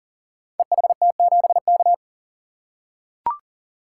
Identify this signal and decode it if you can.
Morse code, heard as a keyed tone, about 40 wpm, 700 Hz. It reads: EHT7K
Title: Code Groups - 5 Letter:Numbers - 40wpm